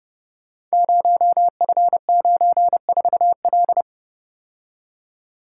Morse code, heard as a keyed tone, about 30 wpm, 700 Hz. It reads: 0F94L
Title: Code Groups - 5 Letter:Numbers - 30wpm